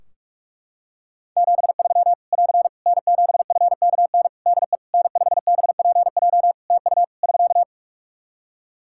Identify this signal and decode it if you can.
Morse code, heard as a keyed tone, about 44 wpm, 700 Hz. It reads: 73 <AR> N7FKN DE N5BPJ TU <SK>
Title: Practice QSO - Exchange 4 - Ending - 45wpm